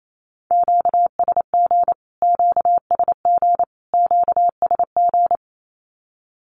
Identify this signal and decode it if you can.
Morse code, heard as a keyed tone, about 28 wpm, 700 Hz. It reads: QSG QSG QSG
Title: Q-Codes Repeated 3x 28wpm